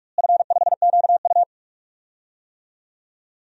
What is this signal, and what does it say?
Morse code, heard as a keyed tone, about 43 wpm, 700 Hz. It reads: F5QU
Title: DXCC-100 Call Signs - 4 Characters - Repeated 3x - 45wpm